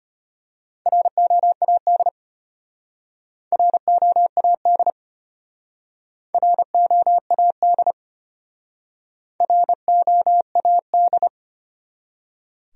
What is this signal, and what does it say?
Morse code, heard as a keyed tone, about 31 wpm, 700 Hz. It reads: ROAD ROAD ROAD ROAD
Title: Speed-Racing Top 500 Words 25wpm